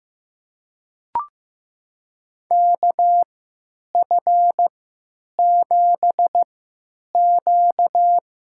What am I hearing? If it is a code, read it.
Morse code, heard as a keyed tone, about 15 wpm, spaced as for 9 wpm, 700 Hz. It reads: KF7Q
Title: USA Call Signs - 3x Character Spacing - 15wpm